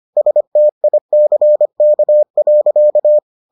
Morse code, heard as a keyed tone, about 25 wpm, 600 Hz. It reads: STICK.